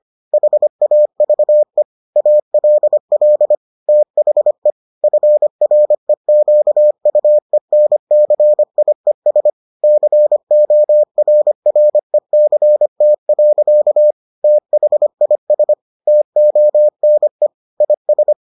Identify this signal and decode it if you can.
Morse code, heard as a keyed tone, about 25 wpm, 600 Hz. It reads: HAVE ALL THE FREQUENCIES CORRECT. THIS TONE IS